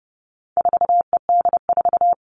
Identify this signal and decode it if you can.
Morse code, heard as a keyed tone, about 30 wpm, 700 Hz. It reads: 4ED4
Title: All Cut Numbers - 4-Digit 30wpm